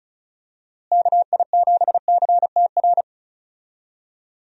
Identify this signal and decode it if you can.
Morse code, heard as a keyed tone, about 36 wpm, 700 Hz. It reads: KI7CTR
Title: USA Call Signs - 6 Characters - Repeated 2x - 35wpm